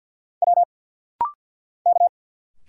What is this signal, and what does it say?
Morse code, heard as a keyed tone, about 50 wpm, 690 Hz. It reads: W K
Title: Single Letters - Rapid-Fire 50wpm